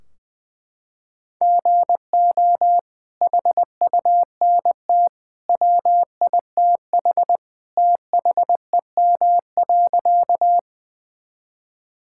Morse code, heard as a keyed tone, about 20 wpm, 700 Hz. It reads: GO HUNT WITH THEM.
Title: Sentences from Top 1600 Words 20wpm